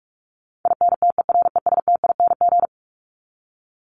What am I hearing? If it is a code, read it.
Morse code, heard as a keyed tone, about 44 wpm, 700 Hz. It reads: INTERESTING
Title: ING Suffix - Top 100 Words with Base Words 45wpm